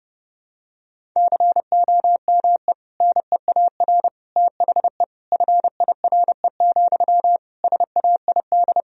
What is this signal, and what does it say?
Morse code, heard as a keyed tone, about 30 wpm, 700 Hz. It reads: COME NEAR THE FIRE, SAID
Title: Sentences from Top 1000 Words - Review Part 1 - 30wpm